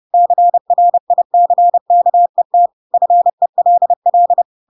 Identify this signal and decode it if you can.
Morse code, heard as a keyed tone, about 30 wpm, 700 Hz. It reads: CRICKET FELL
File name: Pinocchio / Pinocchio ch4 30 30 ik1hge